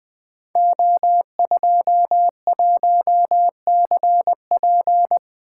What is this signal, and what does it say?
Morse code, heard as a keyed tone, about 20 wpm, 700 Hz. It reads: O21CP
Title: Code Groups - 5 Letter:Numbers - 20wpm